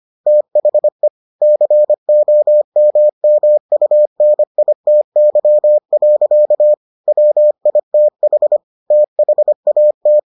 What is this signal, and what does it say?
Morse code, heard as a keyed tone, about 25 wpm, 600 Hz. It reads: THE COMMUNITY. WITH THAT